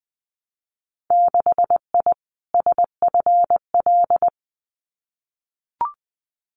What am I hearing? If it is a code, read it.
Morse code, heard as a keyed tone, about 20 wpm, 700 Hz. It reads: 6I SFL
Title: Winter Field Day Exchanges Encore 20wpm